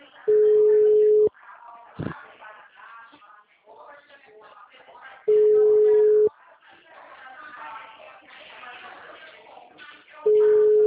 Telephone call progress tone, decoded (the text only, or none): ringback tone